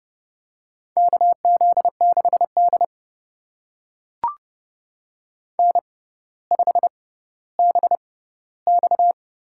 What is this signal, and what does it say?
Morse code, heard as a keyed tone, about 30 wpm, spaced as for 15 wpm, 700 Hz. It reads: KZ6D N5BX
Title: USA Call Signs - 4 Characters - 6x Character Spacing - 30wpm